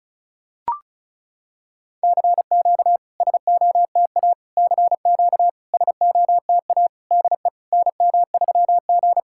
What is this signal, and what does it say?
Morse code, heard as a keyed tone, about 36 wpm, 700 Hz. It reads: CQ SOTA CQ SOTA DE NM3G
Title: SOTA QSO Practice 35wpm